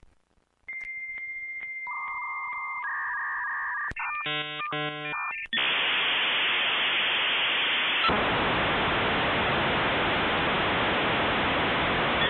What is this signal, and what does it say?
Dial-up modem handshake.